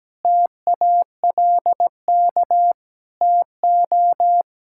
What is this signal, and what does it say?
Morse code, heard as a keyed tone, about 17 wpm, 700 Hz. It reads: TALK TO
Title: Sentences from Top 1000 Words - Review Part 5 - 17wpm